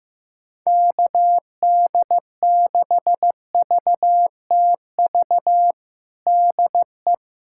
Morse code, heard as a keyed tone, about 16 wpm, 700 Hz. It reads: KD6VTV DE
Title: USA Callsign DE Callsign 15wpm